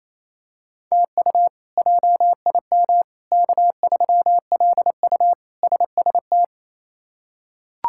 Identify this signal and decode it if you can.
Morse code, heard as a keyed tone, about 28 wpm, 700 Hz. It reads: TU JIM K3LU SST
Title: K1USN SST QSO Practice - 28wpm